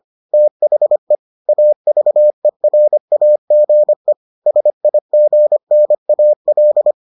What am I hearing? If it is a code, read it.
Morse code, heard as a keyed tone, about 25 wpm, 600 Hz. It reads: THE AVERAGE SIGNAL